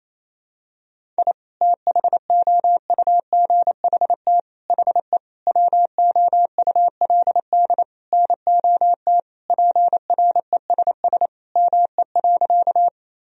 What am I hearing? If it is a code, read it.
Morse code, heard as a keyed tone, about 27 wpm, 700 Hz. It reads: I THOUGHT HE WOULD NOT PRESS ME.
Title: Sentences from Top 1000 Words - Review Part 1 - 28wpm